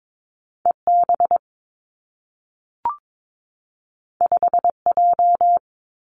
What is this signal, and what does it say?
Morse code, heard as a keyed tone, about 22 wpm, 700 Hz. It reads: E B 5 J
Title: Two Challenging Letter-Number Combinations 22wpm